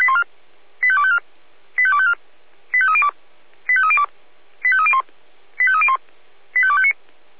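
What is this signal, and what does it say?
Five-tone selective call (ZVEI), heard as quick runs of five tones, tones of about 70 ms. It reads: 85254 85254 85302 85302 85301 85301 85259
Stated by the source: ZVEI five-tone selective call received on an SDR, fire brigade.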